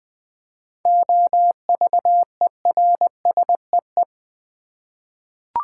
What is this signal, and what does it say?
Morse code, heard as a keyed tone, about 20 wpm, 700 Hz. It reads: OVERSEE
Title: 7 Letter Words - First 3 Letters Given - 20wpm